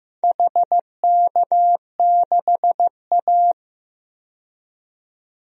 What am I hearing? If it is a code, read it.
Morse code, heard as a keyed tone, about 15 wpm, 700 Hz. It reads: HK6A
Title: DXCC-100 Call Signs - 4 Characters - 15wpm